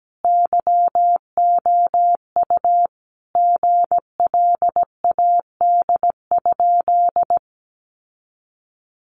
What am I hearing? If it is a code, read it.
Morse code, heard as a keyed tone, about 17 wpm, 700 Hz. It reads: YOU GLAD?
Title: Sentences from Top 800 Words 17wpm